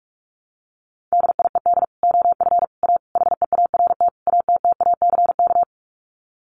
Morse code, heard as a keyed tone, about 44 wpm, 700 Hz. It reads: DIED OF A HEART ATTACK
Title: Sets of 5 Words Encore2 45wpm